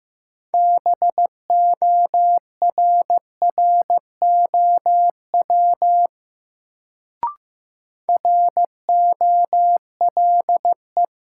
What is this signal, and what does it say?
Morse code, heard as a keyed tone, about 15 wpm, 700 Hz. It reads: BORROW ROLE
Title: Sentences from Top 1500 Words - New Word Drill Alt - 15wpm